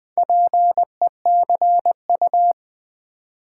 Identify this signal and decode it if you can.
Morse code, heard as a keyed tone, about 20 wpm, 700 Hz. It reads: PECU
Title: Build A Word - 4 Letters At A Time - 8-Letter Words - 20wpm